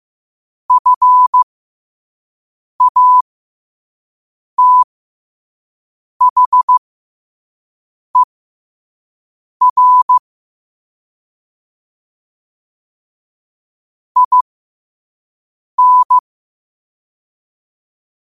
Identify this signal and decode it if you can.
Morse code, heard as a keyed tone, about 15 wpm, spaced as for 5 wpm, 1000 Hz. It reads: FATHER IN
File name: 2024-10-07-5wpm